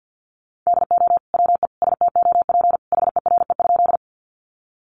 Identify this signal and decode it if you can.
Morse code, heard as a keyed tone, about 50 wpm, 700 Hz. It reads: DO WE STOP HERE?
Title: Sentences from Top 300 Words 50wpm